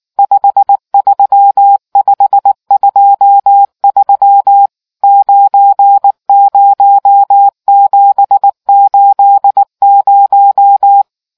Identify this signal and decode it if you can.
Morse code, heard as a keyed tone, about 19 wpm, 800 Hz. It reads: 53523 90780